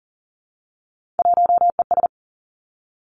Morse code, heard as a keyed tone, about 41 wpm, 690 Hz. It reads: 1ES
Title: Three Challenging Letter-Number Combinations 40wpm